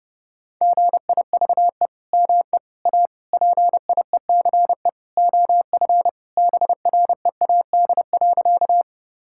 Morse code, heard as a keyed tone, about 32 wpm, 700 Hz. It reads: GIVE ME A PIECE OF BREAD.
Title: Sentences from Top 1600 Words 30wpm